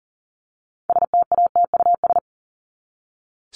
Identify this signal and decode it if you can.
Morse code, heard as a keyed tone, about 40 wpm, 700 Hz. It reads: STATUS
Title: Top 1251-1500 Words 40wpm